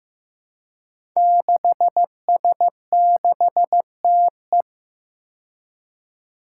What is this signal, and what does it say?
Morse code, heard as a keyed tone, about 15 wpm, 700 Hz. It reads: 6S6TE
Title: Five Challenging Letter-Number Combinations 15wpm